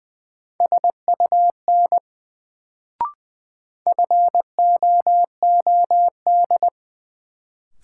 Morse code, heard as a keyed tone, about 20 wpm, 700 Hz. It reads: SUN FOOD
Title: Sentences from Top 300 Words - New Word Drill - 20wpm